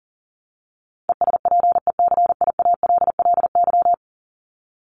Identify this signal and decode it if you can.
Morse code, heard as a keyed tone, about 40 wpm, 700 Hz. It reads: ESPECIALLY
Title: Top 501-750 Words 40wpm